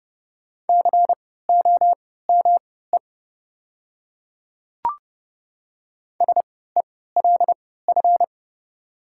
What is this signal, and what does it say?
Morse code, heard as a keyed tone, about 32 wpm, spaced as for 17 wpm, 700 Hz. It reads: COME SELF
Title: 3x Character Spacing - Top 300 Words - 30wpm